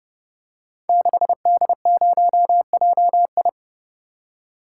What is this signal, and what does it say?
Morse code, heard as a keyed tone, about 28 wpm, 700 Hz. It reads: 6D0JI
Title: Code Groups - 5 Letter:Numbers - 30wpm